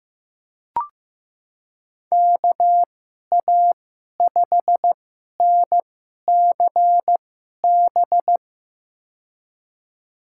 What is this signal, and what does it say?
Morse code, heard as a keyed tone, about 15 wpm, spaced as for 11 wpm, 700 Hz. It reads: KA5NCB
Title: USA Call Signs - 6 Characters - 2x Character Spacing - 15wpm